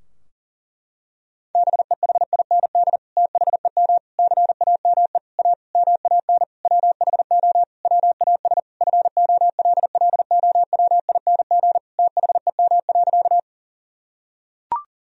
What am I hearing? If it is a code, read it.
Morse code, heard as a keyed tone, about 40 wpm, 700 Hz. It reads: BEHIND THEM CAME A MAN WHO WAS FOLLOWING THEM.
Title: Sentences from Top 1000 Words - Review Part 4 - 40wpm